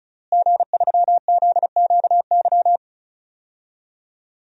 Morse code, heard as a keyed tone, about 34 wpm, 700 Hz. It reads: G3ZQY
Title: DXCC-100 Call Signs - 5 Characters - Repeated 3x - 35wpm